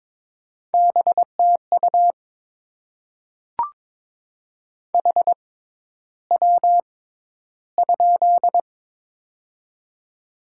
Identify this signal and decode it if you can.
Morse code, heard as a keyed tone, about 22 wpm, spaced as for 11 wpm, 700 Hz. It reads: BTU HW?
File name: Top 20 QSO Elements - 6x Character Spacing - 22wpm